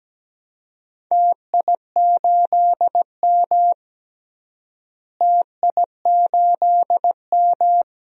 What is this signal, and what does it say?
Morse code, heard as a keyed tone, about 17 wpm, 700 Hz. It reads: TI8M TI8M
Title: DXCC-100 Call Signs - 4 Characters - Repeated 3x - 17wpm